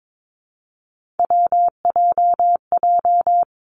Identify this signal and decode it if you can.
Morse code, heard as a keyed tone, about 22 wpm, 700 Hz. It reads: WJJ
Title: Three Challenging Letter-Number Combinations 22wpm